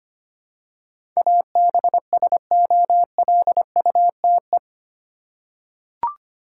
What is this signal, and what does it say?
Morse code, heard as a keyed tone, about 25 wpm, 700 Hz. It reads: ABSOLUTE
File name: Eight Letter Words - 25wpm